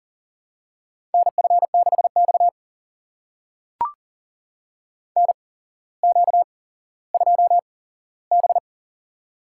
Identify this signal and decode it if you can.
Morse code, heard as a keyed tone, about 40 wpm, spaced as for 17 wpm, 700 Hz. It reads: NF6X NQ2B
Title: USA Call Signs - 4 Characters - 8x Character Spacing - 40wpm